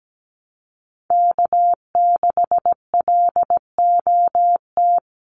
Morse code, heard as a keyed tone, about 17 wpm, 700 Hz. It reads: K6LOT
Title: USA Call Signs Repeated 2x 17wpm